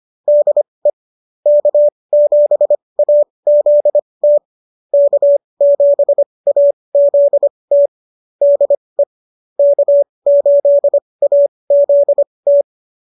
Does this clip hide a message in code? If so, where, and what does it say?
Yes: Morse code from 0.3 to 12.6 s, heard as a keyed tone, about 25 wpm, spaced as for 20 wpm, 590 Hz. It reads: DE K7AZT K7AZT DE K8AZT